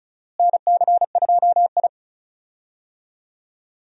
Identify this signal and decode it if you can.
Morse code, heard as a keyed tone, about 36 wpm, 700 Hz. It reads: NC2I
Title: USA Call Signs - 4 Characters - Repeated 2x - 35wpm